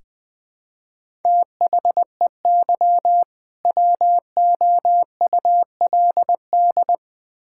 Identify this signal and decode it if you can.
Morse code, heard as a keyed tone, about 20 wpm, 700 Hz. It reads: THEY WOULD
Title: Build A Phrase - 2-Word Phrases - 20wpm